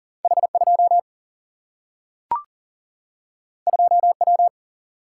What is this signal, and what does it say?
Morse code, heard as a keyed tone, about 40 wpm, 700 Hz. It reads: H 2 2 W
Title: Single-Letter Single-Number 40wpm